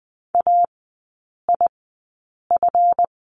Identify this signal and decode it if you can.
Morse code, heard as a keyed tone, about 20 wpm, spaced as for 8 wpm, 700 Hz. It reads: AIF
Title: 191 Lesson 31e - Letter v - Three Characters 20wpm